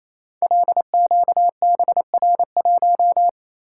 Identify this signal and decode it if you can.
Morse code, heard as a keyed tone, about 29 wpm, 700 Hz. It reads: LQBR1
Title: Code Groups - 5 Letter:Numbers - 28wpm